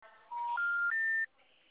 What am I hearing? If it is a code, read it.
Special information tone.